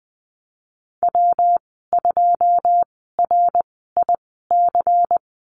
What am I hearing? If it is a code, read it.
Morse code, heard as a keyed tone, about 20 wpm, spaced as for 15 wpm, 700 Hz. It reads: W2RIC
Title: 179 Lesson 29g - Slash - Call Signs 20wpm